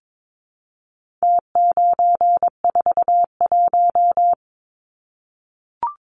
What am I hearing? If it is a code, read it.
Morse code, heard as a keyed tone, about 22 wpm, 700 Hz. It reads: T941
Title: Numbers - Contest Style - 4-Digit 22wpm